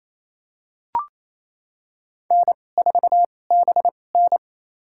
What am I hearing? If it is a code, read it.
Morse code, heard as a keyed tone, about 28 wpm, spaced as for 20 wpm, 700 Hz. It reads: N4BN
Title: USA Call Signs - 2x Character Spacing - 28wpm